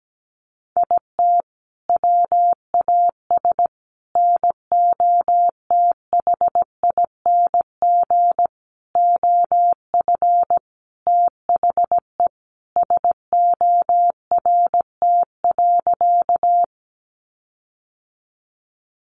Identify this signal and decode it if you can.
Morse code, heard as a keyed tone, about 17 wpm, 700 Hz. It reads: IT WAS NOTHING OF THE SORT.
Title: Sentences from Top 1300 Words 17wpm